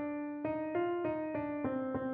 There is no code.